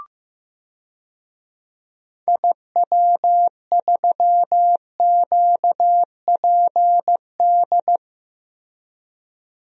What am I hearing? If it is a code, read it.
Morse code, heard as a keyed tone, about 15 wpm, 700 Hz. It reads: IW3QPD
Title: DXCC-100 Call Signs - 6 Characters - Repeated 3x - 15wpm